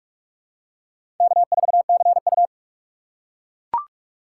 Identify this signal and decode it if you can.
Morse code, heard as a keyed tone, about 44 wpm, 700 Hz. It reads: K4CU
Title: USA Call Signs - 4x Character Spacing - 45wpm